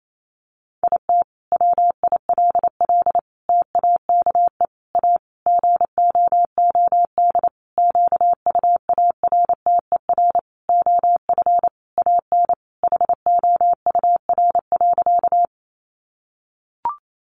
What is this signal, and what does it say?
Morse code, heard as a keyed tone, about 28 wpm, 700 Hz. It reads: IT WILL TAKE A GOOD QUARTER OF AN HOUR.
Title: Sentences from Top 1300 Words 28wpm